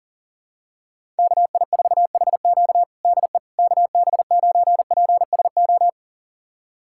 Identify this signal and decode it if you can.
Morse code, heard as a keyed tone, about 40 wpm, 700 Hz. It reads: KI4HQ DE KB9PSO
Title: USA Callsign DE Callsign 40wpm